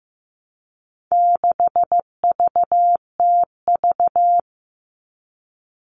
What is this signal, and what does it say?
Morse code, heard as a keyed tone, about 15 wpm, 700 Hz. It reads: 6VTV
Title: All Cut Numbers - 4-Digit 15wpm